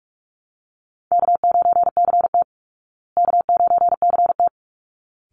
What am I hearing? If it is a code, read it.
Morse code, heard as a keyed tone, about 44 wpm, 700 Hz. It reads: K9CT K9CT
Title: Easy Call Signs Repeated 2x - 45wpm